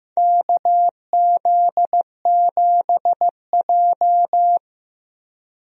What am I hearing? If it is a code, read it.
Morse code, heard as a keyed tone, about 15 wpm, 700 Hz. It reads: KZ7J
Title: USA Call Signs - 4 Characters - Repeated 3x - 15wpm